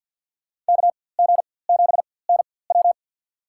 Morse code, heard as a keyed tone, about 47 wpm, spaced as for 23 wpm, 700 Hz. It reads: KG7NW
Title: USA Call Signs - 5 Characters - 4x Character Spacing - 50wpm